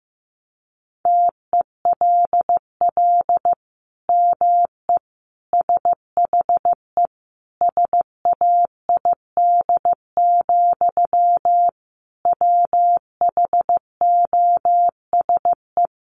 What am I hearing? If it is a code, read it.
Morse code, heard as a keyed tone, about 15 wpm, 700 Hz. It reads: TELL ME SHE SAID, WHOSE
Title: Sentences from Top 1200 Words 15wpm